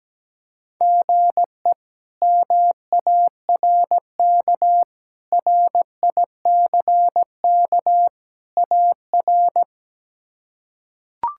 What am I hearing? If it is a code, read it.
Morse code, heard as a keyed tone, about 17 wpm, 700 Hz. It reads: GE MARK RICK AR
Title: K1USN SST QSO Practice - 17wpm